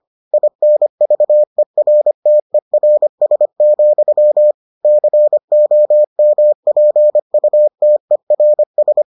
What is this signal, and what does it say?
Morse code, heard as a keyed tone, about 25 wpm, 600 Hz. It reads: INVERTERS, COMPUTERS